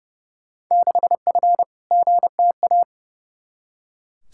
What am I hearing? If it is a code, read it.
Morse code, heard as a keyed tone, about 30 wpm, 700 Hz. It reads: 6F GTA
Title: Field Day Exchanges Encore - 30wpm